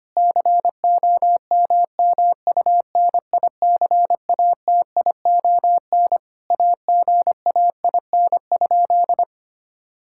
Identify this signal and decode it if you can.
Morse code, heard as a keyed tone, about 25 wpm, 700 Hz. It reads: COMMUNICATION AGAIN?
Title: Sentences from Top 1000 Words - Review Part 6 - 25wpm